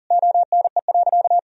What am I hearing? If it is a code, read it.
Morse code, heard as a keyed tone, about 40 wpm, 700 Hz. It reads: ONE.
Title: Sentences from Top 1000 Words - Review Part 3 - 40wpm